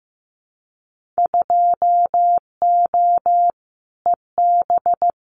Morse code, heard as a keyed tone, about 15 wpm, 700 Hz. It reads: 2O EB
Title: Winter Field Day Exchanges Encore 15wpm